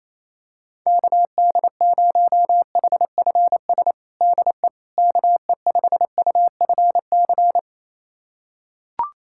Morse code, heard as a keyed tone, about 28 wpm, 700 Hz. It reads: KD0HFS DE KE5UFC
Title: USA Callsign DE Callsign - 6 Characters - 28wpm